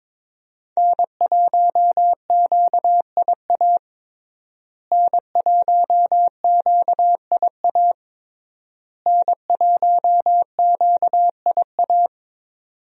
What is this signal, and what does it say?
Morse code, heard as a keyed tone, about 22 wpm, 700 Hz. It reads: N1QIA N1QIA N1QIA
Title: USA Call Signs - 5 Characters - Repeated 3x - 22wpm